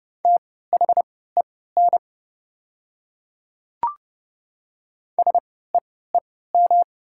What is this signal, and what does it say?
Morse code, heard as a keyed tone, about 32 wpm, spaced as for 17 wpm, 700 Hz. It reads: THEN SEEM
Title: 3x Character Spacing - Top 500 Words - 30wpm